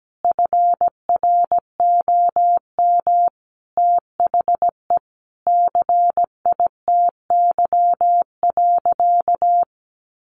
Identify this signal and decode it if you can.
Morse code, heard as a keyed tone, about 17 wpm, 700 Hz. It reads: FROM THE CITY.
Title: Sentences from Top 1200 Words 17wpm